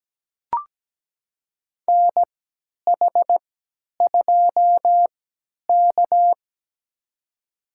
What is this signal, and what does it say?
Morse code, heard as a keyed tone, about 17 wpm, spaced as for 10 wpm, 700 Hz. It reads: NH2K
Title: USA Call Signs - 4 Characters - 3x Character Spacing - 17wpm